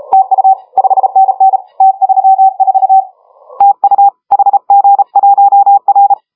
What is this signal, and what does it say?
Morse code, heard as a keyed tone, about 34 wpm, 810 Hz. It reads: TU 5NN T34 TU HG1R
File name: HG1R CW SSB MIX